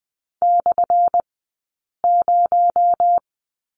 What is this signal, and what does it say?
Morse code, heard as a keyed tone, about 20 wpm, spaced as for 8 wpm, 690 Hz. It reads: /0
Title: 217 Lesson 35d - Number 0 - Two Characters 20wpm